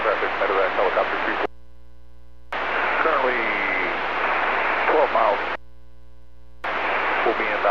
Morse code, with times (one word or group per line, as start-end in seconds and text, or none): none